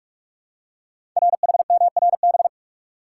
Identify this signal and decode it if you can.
Morse code, heard as a keyed tone, about 45 wpm, 700 Hz. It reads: RHMRB
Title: Code Groups - 5 Letters - 45wpm